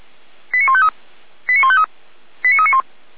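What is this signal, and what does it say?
Five-tone selective call (ZVEI), heard as quick runs of five tones, tones of about 70 ms. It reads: 89153 89153 89391
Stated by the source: ZVEI five-tone selective call received by SDR.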